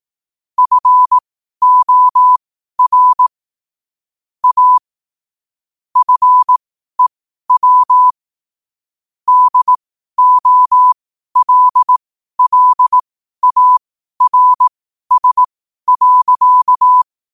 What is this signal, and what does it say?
Morse code, heard as a keyed tone, about 18 wpm, spaced as for 12 wpm, 1000 Hz. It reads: FOR A FEW DOLLARS.
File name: qotdfiles / 12wpm